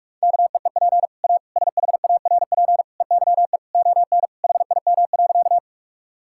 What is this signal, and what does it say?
Morse code, heard as a keyed tone, about 46 wpm, 700 Hz. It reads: KEEP A SHARP EYE ON HIM.
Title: Sentences from Top 1600 Words 45wpm